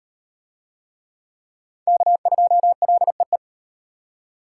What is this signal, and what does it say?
Morse code, heard as a keyed tone, about 38 wpm, 700 Hz. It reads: K2LEE